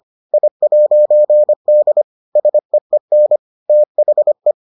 Morse code, heard as a keyed tone, about 25 wpm, 600 Hz. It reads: I'D SEEN THE